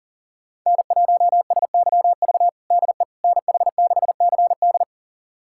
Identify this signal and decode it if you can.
Morse code, heard as a keyed tone, about 37 wpm, 700 Hz. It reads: N1SYV DE NH6CD
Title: USA Callsign DE Callsign - 5 Characters - 40wpm